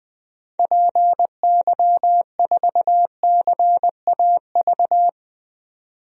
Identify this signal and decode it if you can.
Morse code, heard as a keyed tone, about 20 wpm, 700 Hz. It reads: PY4CAV
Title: DXCC-100 Call Signs - 6 Characters - 20wpm